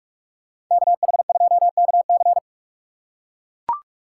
Morse code, heard as a keyed tone, about 44 wpm, 700 Hz. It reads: KH2KC
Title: USA Call Signs - 5 Characters - 6x Character Spacing - 45wpm